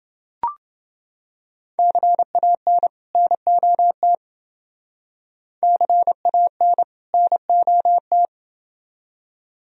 Morse code, heard as a keyed tone, about 29 wpm, 700 Hz. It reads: CAN NOT CAN NOT